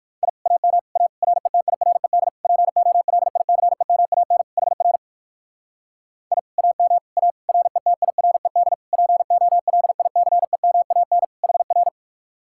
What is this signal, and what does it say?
Morse code, heard as a keyed tone, about 58 wpm, 700 Hz. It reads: I AM A RETIRED POLICEMAN HR I AM A RETIRED POLICEMAN HR